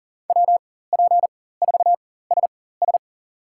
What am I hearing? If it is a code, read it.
Morse code, heard as a keyed tone, about 40 wpm, spaced as for 19 wpm, 700 Hz. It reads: WP4SS
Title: USA Call Signs - 5 Characters - 4x Character Spacing - 40wpm